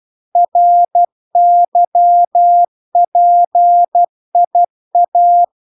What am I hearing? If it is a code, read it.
Morse code, heard as a keyed tone, about 12 wpm, 700 Hz. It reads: RYPIA